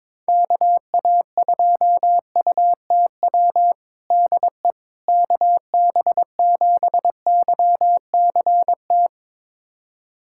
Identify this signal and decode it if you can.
Morse code, heard as a keyed tone, about 22 wpm, 700 Hz. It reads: KA2UTW DE KB7YCT
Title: USA Callsign DE Callsign 22wpm